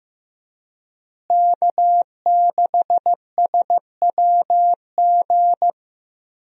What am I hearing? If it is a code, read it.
Morse code, heard as a keyed tone, about 15 wpm, 700 Hz. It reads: K6SWG
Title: USA Call Signs - 5 Characters - Repeated 3x - 15wpm